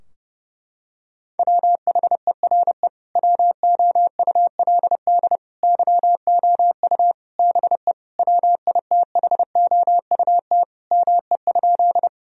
Morse code, heard as a keyed tone, about 32 wpm, 700 Hz. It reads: WHERE WOULD YOU BE WITHOUT ME?
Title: Sentences from Top 1000 Words - Review Part 3 - 30wpm